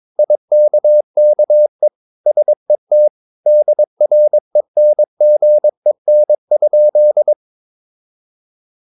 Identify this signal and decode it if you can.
Morse code, heard as a keyed tone, about 22 wpm, 600 Hz. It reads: IKKE SET DRENGEN?
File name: Ferie QRS 22wpm